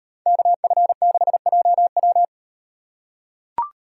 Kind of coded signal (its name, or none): Morse code